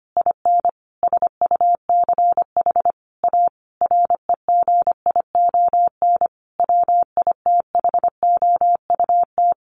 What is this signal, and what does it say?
Morse code, heard as a keyed tone, about 25 wpm, 700 Hz. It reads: IN SUCH A REGION WITHOUT
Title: Sentences from Top 1500 Words 25wpm